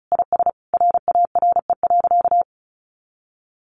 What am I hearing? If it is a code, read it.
Morse code, heard as a keyed tone, about 34 wpm, 700 Hz. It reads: IS RARE.
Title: Sentences from Top 1100 Words 35wpm